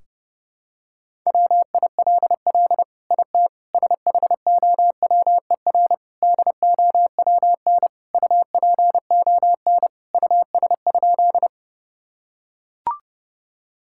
Morse code, heard as a keyed tone, about 30 wpm, 700 Hz. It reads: WILL IT SHOWER DOWN UPON US?
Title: Sentences from Top 1600 Words 30wpm